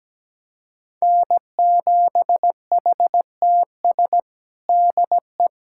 Morse code, heard as a keyed tone, about 17 wpm, 700 Hz. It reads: N7HTS DE
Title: USA Callsign DE Callsign 17wpm